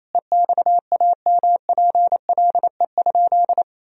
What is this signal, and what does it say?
Morse code, heard as a keyed tone, about 28 wpm, 700 Hz. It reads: EXAMPLE?
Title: Sentences from Top 1000 Words - Review Part 1 - 28wpm